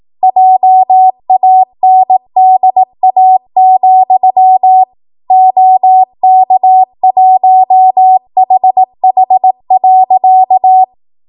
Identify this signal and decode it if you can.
Morse code, heard as a keyed tone, about 18 wpm, 750 Hz. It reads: JANDA, OK1HH.
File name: Morse / 18WPM